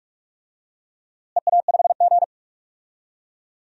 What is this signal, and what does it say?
Morse code, heard as a keyed tone, about 41 wpm, 700 Hz. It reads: EA5G
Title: DXCC-100 Call Signs - 4 Characters - Repeated 2x - 45wpm